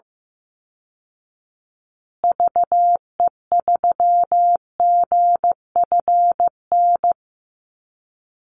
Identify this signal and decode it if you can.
Morse code, heard as a keyed tone, about 15 wpm, 700 Hz. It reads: VE3GFN
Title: Call Signs Repeated 2x - 15wpm